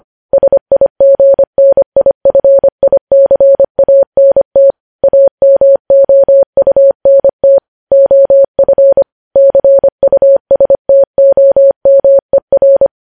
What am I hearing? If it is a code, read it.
Morse code, heard as a keyed tone, about 25 wpm, 550 Hz. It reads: SIGNIFICANT AMOUNT OF CUSTOMER
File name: podcast / News headlines-1x-25x25